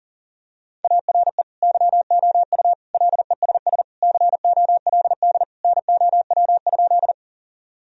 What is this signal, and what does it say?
Morse code, heard as a keyed tone, about 40 wpm, 700 Hz. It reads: ARE YOU LESS COLD NOW?
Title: Sentences from Top 1000 Words - Review Part 2 - 40wpm